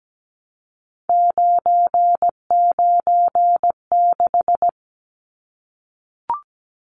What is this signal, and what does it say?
Morse code, heard as a keyed tone, about 17 wpm, 700 Hz. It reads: 996
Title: Numbers - 3-Digit 17wpm